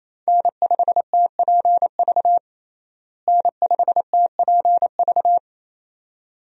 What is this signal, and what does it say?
Morse code, heard as a keyed tone, about 28 wpm, 700 Hz. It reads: N5TPV N5TPV
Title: USA Call Signs - 5 Characters - Repeated 2x - 28wpm